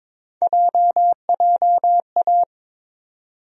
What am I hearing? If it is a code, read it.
Morse code, heard as a keyed tone, about 22 wpm, 700 Hz. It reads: JJA
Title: USA Call Sign Postfixes - 3 Letters 22wpm